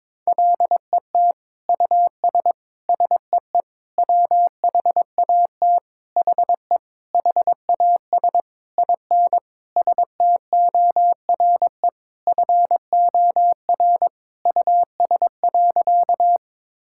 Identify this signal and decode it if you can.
Morse code, heard as a keyed tone, about 22 wpm, 700 Hz. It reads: LET US SEE WHAT HE HAS IN STORE FOR US.
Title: Sentences from Top 1000 Words - Review Part 5 - 22wpm